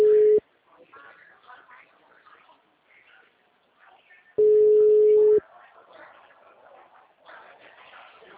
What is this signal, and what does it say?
Ringback tone (Europe, 425 Hz).
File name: out- -2005- - - .60382